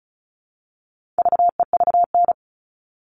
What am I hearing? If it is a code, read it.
Morse code, heard as a keyed tone, about 34 wpm, 700 Hz. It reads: VEVN
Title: All Cut Numbers - 4-Digit 35wpm